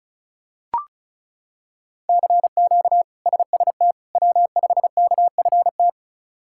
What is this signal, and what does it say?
Morse code, heard as a keyed tone, about 34 wpm, 700 Hz. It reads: CQ SST W5KFT
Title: K1USN SST QSO Practice - 35wpm